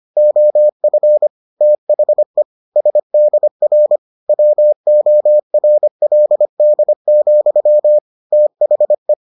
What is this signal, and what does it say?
Morse code, heard as a keyed tone, about 25 wpm, 600 Hz. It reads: OF THE SDR WORLD, THE